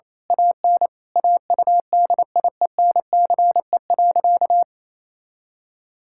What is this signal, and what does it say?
Morse code, heard as a keyed tone, about 28 wpm, 700 Hz. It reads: AN AUDIENCE.
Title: Sentences from Top 1000 Words - Review Part 5 - 28wpm